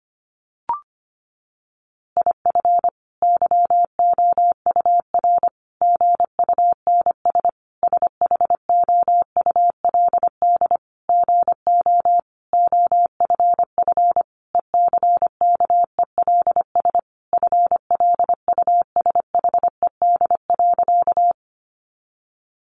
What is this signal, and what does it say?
Morse code, heard as a keyed tone, about 25 wpm, 700 Hz. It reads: IF YOUR GUNS SHOULD GO OFF ECKELS FLUSHED.
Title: A Sound Of Thunder 25wpm